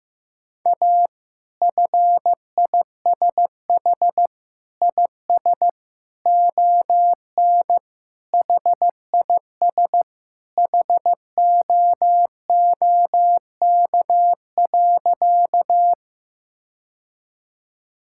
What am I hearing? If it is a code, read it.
Morse code, heard as a keyed tone, about 15 wpm, 700 Hz. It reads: A FISH IS ON HIS HOOK.
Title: Sentences from Top 1600 Words 15wpm